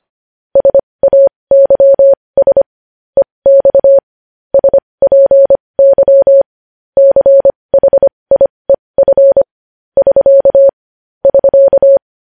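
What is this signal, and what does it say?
Morse code, heard as a keyed tone, about 25 wpm, spaced as for 20 wpm, 550 Hz. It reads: SAYS EX SPY CHIEF <SK> <SK>
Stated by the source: Morse news headlines sent at 20.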